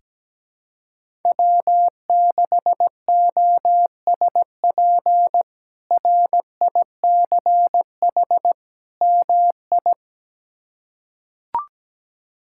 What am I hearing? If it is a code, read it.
Morse code, heard as a keyed tone, about 17 wpm, 700 Hz. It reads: W6OSP RICH MI
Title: K1USN SST QSO Practice - 17wpm